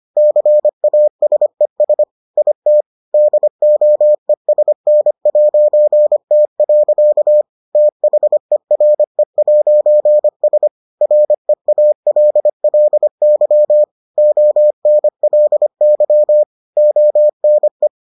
Morse code, heard as a keyed tone, about 25 wpm, 600 Hz. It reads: CASES IT DOESN'T. THERE'S REALLY ONLY ONE